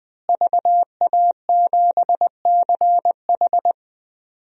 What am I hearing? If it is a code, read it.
Morse code, heard as a keyed tone, about 20 wpm, 700 Hz. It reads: VA7CH
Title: DXCC-100 Call Signs - 5 Characters - Repeated 2x - 20wpm